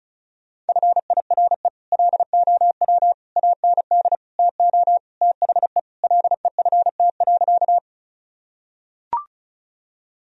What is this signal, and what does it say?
Morse code, heard as a keyed tone, about 34 wpm, 700 Hz. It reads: FIRE LOW AND TO THE LEFT.
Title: Sentences from Top 1000 Words - Review Part 1 - 35wpm